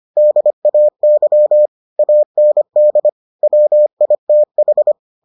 Morse code, heard as a keyed tone, about 25 wpm, 600 Hz. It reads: DAY AND WITH